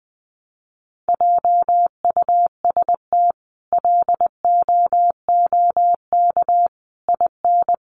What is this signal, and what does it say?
Morse code, heard as a keyed tone, about 20 wpm, 700 Hz. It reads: JUST LOOK IN
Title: Sentences from Top 1300 Words 20wpm